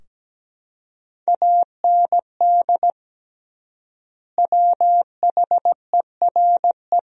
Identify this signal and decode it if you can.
Morse code, heard as a keyed tone, about 17 wpm, 700 Hz. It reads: AND WHERE
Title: 3x Word-Spacing - Sentences from Top 200 Words - 17wpm